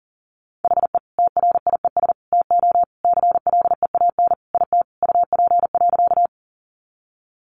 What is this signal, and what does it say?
Morse code, heard as a keyed tone, about 40 wpm, 700 Hz. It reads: HE TRIES TO CLEAN IT UP.
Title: Additional Easy Phrases 40wpm